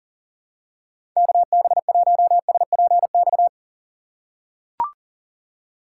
Morse code, heard as a keyed tone, about 40 wpm, 700 Hz. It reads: KB1SPX
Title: USA Call Signs - 6x Character Spacing - 40wpm